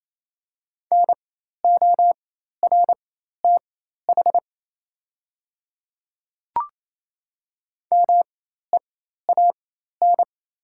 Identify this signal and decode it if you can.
Morse code, heard as a keyed tone, about 28 wpm, spaced as for 13 wpm, 700 Hz. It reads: NORTH MEAN
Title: 4x Character Spacing - Top 300 Words - 28wpm